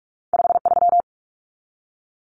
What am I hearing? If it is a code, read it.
Morse code, heard as a keyed tone, about 46 wpm, 700 Hz. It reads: 53
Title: Common Cut Numbers - 2-Digit 45wpm